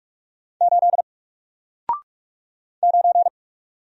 Morse code, heard as a keyed tone, about 46 wpm, 700 Hz. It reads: 8 9
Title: Single Letter-Number Alt 45wpm